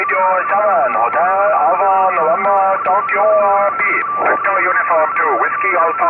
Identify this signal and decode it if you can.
Two-tone teleprinter signal, about 170 Hz shift, with speech.
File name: 2010 tape BCC Challenge Saturday